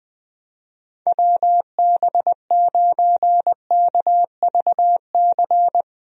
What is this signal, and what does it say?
Morse code, heard as a keyed tone, about 21 wpm, 700 Hz. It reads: WB9KVC
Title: USA Call Signs - 6 Characters - Repeated 3x - 20wpm